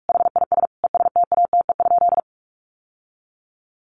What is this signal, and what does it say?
Morse code, heard as a keyed tone, about 44 wpm, 700 Hz. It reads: HIS ESTATE?
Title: Sentences from Top 1200 Words 45wpm